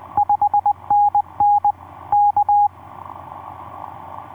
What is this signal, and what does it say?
Morse code, heard as a keyed tone, about 20 wpm, 820 Hz. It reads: 5NN K
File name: v5-dk1ce sq7nsp etc 12m cw 02jan2011 1320